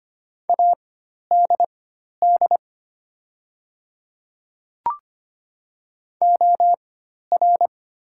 Morse code, heard as a keyed tone, about 25 wpm, spaced as for 12 wpm, 700 Hz. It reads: ADD OR
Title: 4x Character Spacing - Top 500 Words - 25wpm